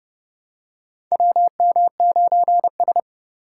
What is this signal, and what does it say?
Morse code, heard as a keyed tone, about 30 wpm, 700 Hz. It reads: WM9S
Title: USA Call Signs - 4 Characters - 3x Character Spacing - 30wpm